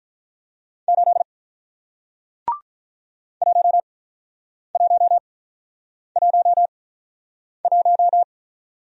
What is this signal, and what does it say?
Morse code, heard as a keyed tone, about 48 wpm, spaced as for 9 wpm, 700 Hz. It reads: 8 1111